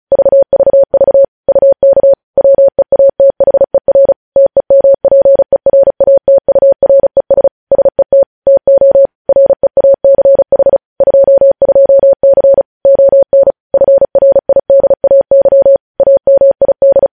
Morse code, heard as a keyed tone, about 36 wpm, 550 Hz. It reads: VVV UK WEATHER TEMPERATURES SET TO REACH 22C ON FRIDAY AMID
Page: news headlines in morse code at 35 wpm